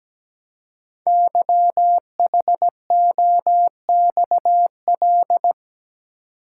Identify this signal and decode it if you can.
Morse code, heard as a keyed tone, about 17 wpm, 700 Hz. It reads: YHOXL
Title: Code Groups - 5 Letters - 17wpm